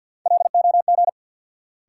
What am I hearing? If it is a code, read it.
Morse code, heard as a keyed tone, about 48 wpm, 700 Hz. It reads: LOG
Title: Sentences from Top 1400 Words - New Word Drill Alt - 50wpm